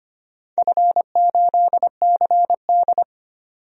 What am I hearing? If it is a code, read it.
Morse code, heard as a keyed tone, about 25 wpm, 700 Hz. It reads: F8CD
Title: DXCC-100 Call Signs - 4 Characters - 25wpm